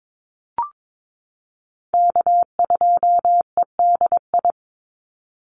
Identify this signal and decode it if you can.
Morse code, heard as a keyed tone, about 22 wpm, 700 Hz. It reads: K2EDI
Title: USA Call Signs 22wpm